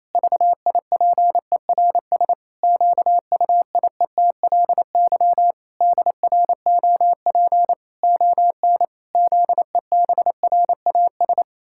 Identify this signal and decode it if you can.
Morse code, heard as a keyed tone, about 28 wpm, 700 Hz. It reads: VIPERS QUIETLY DROP ON ZEBRAS
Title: Pangram Challenge v2 - 28wpm